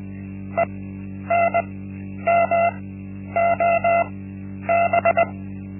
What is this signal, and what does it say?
Morse code, heard as a keyed tone, about 20 wpm, spaced as for 10 wpm, 680 Hz. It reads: ENMOB
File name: Linksmakalnis-1